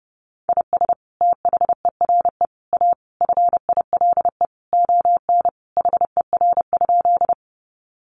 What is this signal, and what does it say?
Morse code, heard as a keyed tone, about 30 wpm, 700 Hz. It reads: IS THERE A FILE ON HER?
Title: Sentences from Top 1200 Words 30wpm